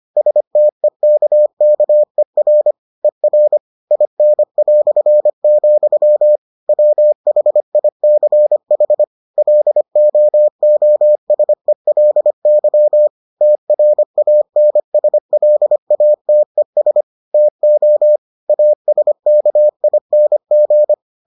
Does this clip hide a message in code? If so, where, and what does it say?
Yes: Morse code from 0.2 to 20.9 s, heard as a keyed tone, about 25 wpm, 600 Hz. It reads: STEKKER ER IN", WHICH LOOSELY TRANSLATES TO ASKING